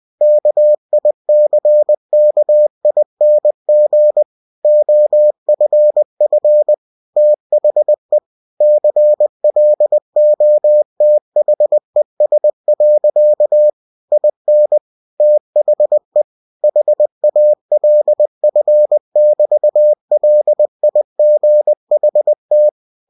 Morse code, heard as a keyed tone, about 20 wpm, 600 Hz. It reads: KICKING OFF THE CLOTHES. IN THE HALF<BT>LIGHT
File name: cw-books / winnie-the-pooh / winnie-20wpm-page-0040